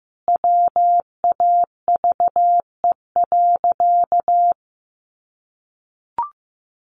Morse code, heard as a keyed tone, about 15 wpm, 700 Hz. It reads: WAVE.
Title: Sentences from Top 1400 Words 15wpm